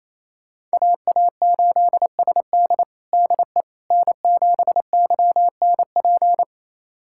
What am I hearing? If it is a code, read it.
Morse code, heard as a keyed tone, about 28 wpm, 700 Hz. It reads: AA8SD DE N7YNP
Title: USA Callsign DE Callsign - 5 Characters - 28wpm